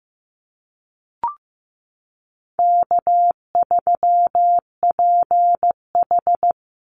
Morse code, heard as a keyed tone, about 15 wpm, 700 Hz. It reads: K3PH
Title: Easy Call Signs Repeated 3x - 15wpm